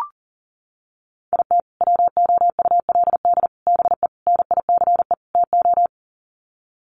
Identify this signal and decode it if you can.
Morse code, heard as a keyed tone, about 40 wpm, 700 Hz. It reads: IT WOULD BE NICE TO
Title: Sets of 5 Words Encore2 40wpm